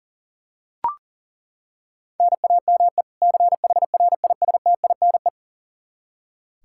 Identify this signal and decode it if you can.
Morse code, heard as a keyed tone, about 37 wpm, 700 Hz. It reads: NAME CHRISTINE
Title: Practice QSO - Exchange 1 - Beginning - 40wpm